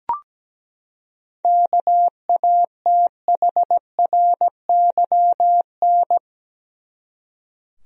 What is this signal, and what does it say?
Morse code, heard as a keyed tone, about 17 wpm, 700 Hz. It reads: KATHRYN
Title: USA Common Names - 17wpm